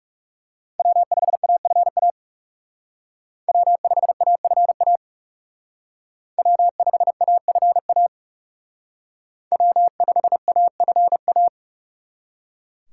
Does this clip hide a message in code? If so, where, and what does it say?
Yes: Morse code from 0.8 to 11.5 s, heard as a keyed tone, about 38 wpm, 700 Hz. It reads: W5AFA W5AFA W5AFA W5AFA